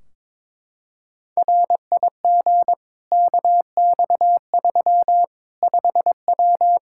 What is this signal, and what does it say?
Morse code, heard as a keyed tone, about 22 wpm, 700 Hz. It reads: RIG KX3 5W
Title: Practice QSO - Exchange 2 - Info - 22wpm